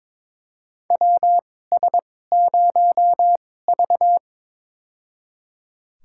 Morse code, heard as a keyed tone, about 22 wpm, spaced as for 16 wpm, 700 Hz. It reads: WS0V
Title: USA Call Signs - 4 Characters - 2x Character Spacing - 22wpm